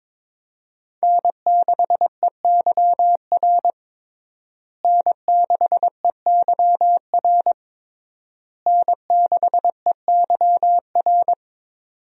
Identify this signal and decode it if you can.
Morse code, heard as a keyed tone, about 22 wpm, 700 Hz. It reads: N6EYR N6EYR N6EYR
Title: USA Call Signs - 5 Characters - Repeated 3x - 22wpm